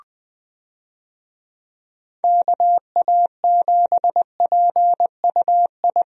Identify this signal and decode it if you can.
Morse code, heard as a keyed tone, about 20 wpm, 700 Hz. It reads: KA7PUI
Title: USA Call Signs - 6 Characters - Repeated 3x - 20wpm